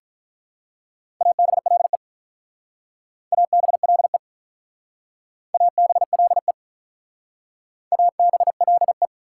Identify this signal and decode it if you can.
Morse code, heard as a keyed tone, about 47 wpm, 690 Hz. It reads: ABLE ABLE ABLE ABLE